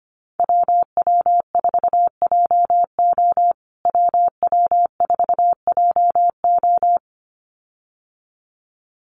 Morse code, heard as a keyed tone, about 25 wpm, 700 Hz. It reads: WW4JO WW4JO
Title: USA Call Signs Repeated 2x 25wpm